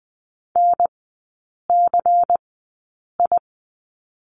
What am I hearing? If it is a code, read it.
Morse code, heard as a keyed tone, about 20 wpm, spaced as for 8 wpm, 700 Hz. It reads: NCI
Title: 100 Lesson 17e - Letter u - Three Characters 20wpm